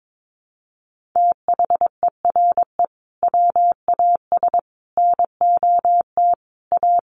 Morse code, heard as a keyed tone, about 22 wpm, 700 Hz. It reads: THERE WAS NOT A
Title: Sentences from Top 1500 Words 22wpm